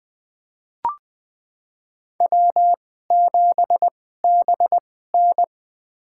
Morse code, heard as a keyed tone, about 20 wpm, spaced as for 14 wpm, 700 Hz. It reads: W7BN
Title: USA Call Signs - 4 Characters - 2x Character Spacing - 20wpm